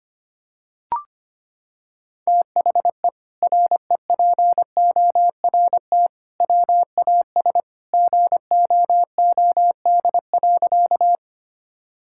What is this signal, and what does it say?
Morse code, heard as a keyed tone, about 25 wpm, 700 Hz. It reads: THE REPORT WAS GOOD.
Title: Sentences from Top 1000 Words - Review Part 6 - 25wpm